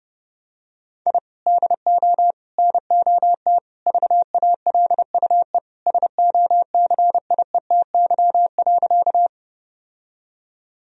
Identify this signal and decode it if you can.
Morse code, heard as a keyed tone, about 32 wpm, 700 Hz. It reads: I DO NOT VALUE SOCIETY.
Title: Sentences from Top 1000 Words - Review Part 4 - 30wpm